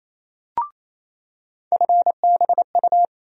Morse code, heard as a keyed tone, about 28 wpm, 700 Hz. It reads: FBU
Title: USA Call Sign Postfixes - 3 Letters 28wpm